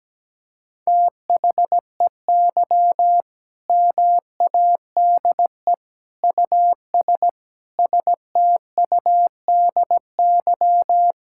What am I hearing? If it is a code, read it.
Morse code, heard as a keyed tone, about 17 wpm, 700 Hz. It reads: THEY MADE US STUDY
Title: Sentences from Top 1000 Words - Review Part 6 - 17wpm